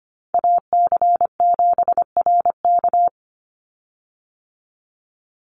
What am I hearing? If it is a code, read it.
Morse code, heard as a keyed tone, about 25 wpm, 700 Hz. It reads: AC7RK
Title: USA Call Signs 25wpm